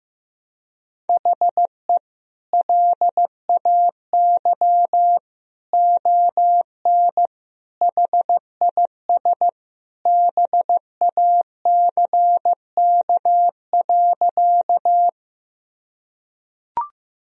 Morse code, heard as a keyed tone, about 15 wpm, 700 Hz. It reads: HE LAY ON HIS BACK.
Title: Sentences from Top 1000 Words - Review Part 2 - 15wpm